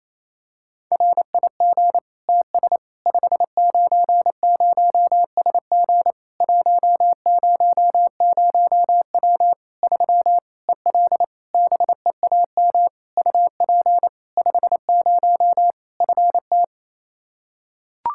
Morse code, heard as a keyed tone, about 27 wpm, 700 Hz. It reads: RIG TS 590SG 100W 3 EL BEAM UP 50 FT
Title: K1USN SST QSO Practice 28wpm